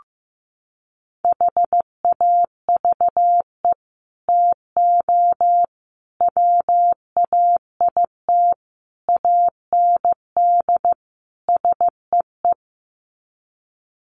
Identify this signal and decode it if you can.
Morse code, heard as a keyed tone, about 15 wpm, 700 Hz. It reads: HAVE TO WAIT AND SEE
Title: Sets of 5 Words Encore2 15wpm